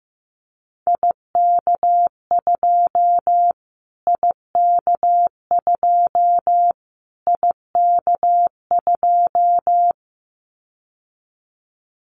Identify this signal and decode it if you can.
Morse code, heard as a keyed tone, about 15 wpm, 700 Hz. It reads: IK2 IK2 IK2
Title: Callsign Prefixes for Top 25 DXCC - Repeated 3x - 15wpm